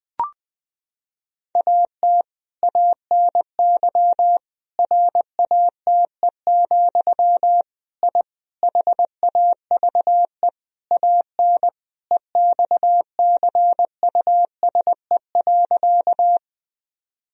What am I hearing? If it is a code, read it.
Morse code, heard as a keyed tone, about 21 wpm, 700 Hz. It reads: AT ANY RATE, I HAVE AN EXCUSE.
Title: Sentences from Top 1000 Words - Review Part 4 - 20wpm